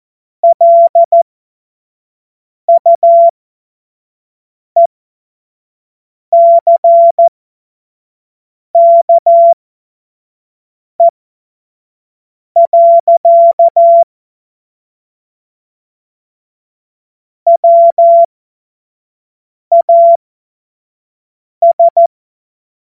Morse code, heard as a keyed tone, about 14 wpm, spaced as for 5 wpm, 680 Hz. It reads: LUECKE. WAS